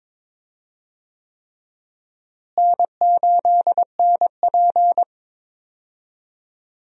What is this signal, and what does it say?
Morse code, heard as a keyed tone, about 22 wpm, 700 Hz. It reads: N8NP
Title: USA Call Signs - 4 Characters - Repeated 3x - 22wpm